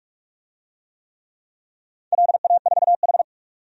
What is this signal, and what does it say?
Morse code, heard as a keyed tone, about 46 wpm, 700 Hz. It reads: LA4H
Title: DXCC-100 Call Signs - 4 Characters - 45wpm